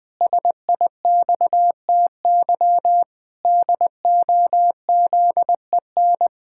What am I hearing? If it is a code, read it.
Morse code, heard as a keyed tone, about 20 wpm, 700 Hz. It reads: SIXTY DOZEN
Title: Pangram Challenge v2 - 20wpm